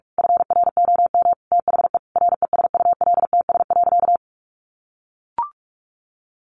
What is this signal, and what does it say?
Morse code, heard as a keyed tone, about 44 wpm, 700 Hz. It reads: FROM THE RESULTS.
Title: A Sound Of Thunder 45wpm